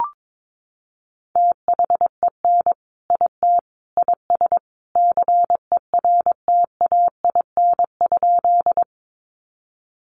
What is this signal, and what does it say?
Morse code, heard as a keyed tone, about 22 wpm, 700 Hz. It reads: THEN IT IS CERTAIN?
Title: Sentences from Top 500 Words 22wpm